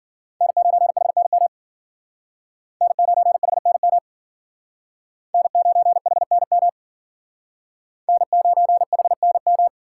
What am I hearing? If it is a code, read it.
Morse code, heard as a keyed tone, about 52 wpm, 700 Hz. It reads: N9HNM N9HNM N9HNM N9HNM